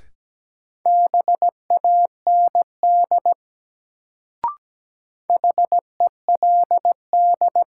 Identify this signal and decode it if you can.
Morse code, heard as a keyed tone, about 17 wpm, 700 Hz. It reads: BAND HELD
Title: Sentences from Top 1600 Words - New Word Drill Alt - 17wpm